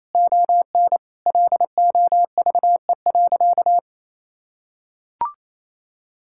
Morse code, heard as a keyed tone, about 28 wpm, 700 Hz. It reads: ON LOVE.
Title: Sentences from Top 1000 Words - Review Part 6 - 28wpm